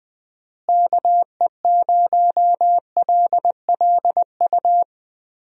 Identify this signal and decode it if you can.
Morse code, heard as a keyed tone, about 21 wpm, 700 Hz. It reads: KE0LLU
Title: USA Call Signs - 6 Characters - Repeated 3x - 20wpm